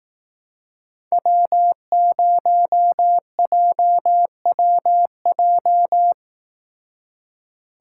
Morse code, heard as a keyed tone, about 17 wpm, 700 Hz. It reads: W0JWJ
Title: Speed Racing - USA 1x3 Call Signs 15wpm